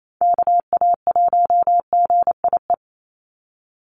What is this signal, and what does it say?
Morse code, heard as a keyed tone, about 28 wpm, 700 Hz. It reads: KA1GIE
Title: USA Call Signs 28wpm